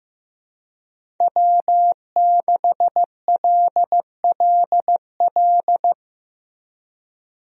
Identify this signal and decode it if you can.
Morse code, heard as a keyed tone, about 15 wpm, 700 Hz. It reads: W6LLL
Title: USA Call Signs - 5 Characters - Repeated 3x - 15wpm